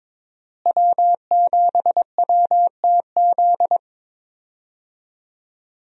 Morse code, heard as a keyed tone, about 22 wpm, 700 Hz. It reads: W7WTZ
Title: USA Call Signs - 5 Characters - Repeated 3x - 22wpm